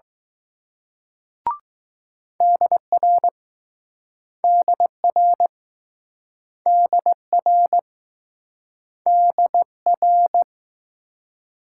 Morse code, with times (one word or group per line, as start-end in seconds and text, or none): 2.4-3.3 DR
4.4-5.5 DR
6.7-7.8 DR
9.1-10.4 DR